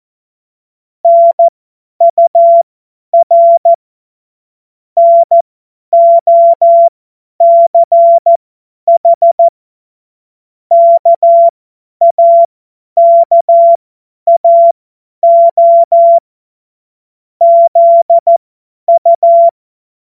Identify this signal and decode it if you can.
Morse code, heard as a keyed tone, about 14 wpm, spaced as for 10 wpm, 680 Hz. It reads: NUR NOCH KAKAO ZU
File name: Morsekurs / Witze / Witz 07. März 2023 50 F70